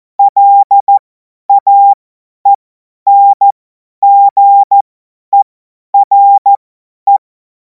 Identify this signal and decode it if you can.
Morse code, heard as a keyed tone, about 14 wpm, spaced as for 10 wpm, 800 Hz. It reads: LAENGERE